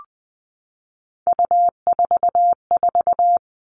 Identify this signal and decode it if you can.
Morse code, heard as a keyed tone, about 20 wpm, 700 Hz. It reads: U44
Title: All Cut Numbers - 3-Digit 20wpm